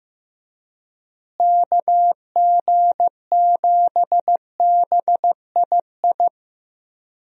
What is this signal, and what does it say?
Morse code, heard as a keyed tone, about 15 wpm, 700 Hz. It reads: KG7BII
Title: USA Call Signs - 6 Characters - Repeated 2x - 15wpm